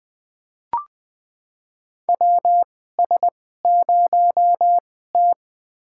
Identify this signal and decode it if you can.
Morse code, heard as a keyed tone, about 20 wpm, spaced as for 15 wpm, 700 Hz. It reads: WS0T
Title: USA Call Signs - 2x Character Spacing - 20wpm